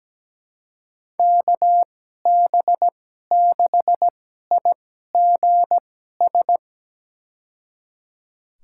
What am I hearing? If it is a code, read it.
Morse code, heard as a keyed tone, about 17 wpm, spaced as for 12 wpm, 700 Hz. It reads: KB6IGS
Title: USA Call Signs - 2x Character Spacing - 17wpm